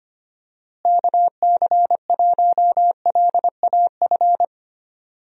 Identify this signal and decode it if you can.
Morse code, heard as a keyed tone, about 25 wpm, 700 Hz. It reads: KC1LAF
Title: USA Call Signs - 6 Characters - Repeated 3x - 25wpm